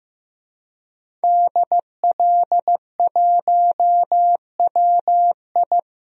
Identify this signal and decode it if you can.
Morse code, heard as a keyed tone, about 15 wpm, 700 Hz. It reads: DL1WI
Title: DXCC-100 Call Signs - 5 Characters - 15wpm